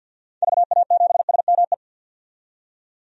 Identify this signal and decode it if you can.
Morse code, heard as a keyed tone, about 45 wpm, 700 Hz. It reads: VA7SME
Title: DXCC-100 Call Signs - 6 Characters - 50wpm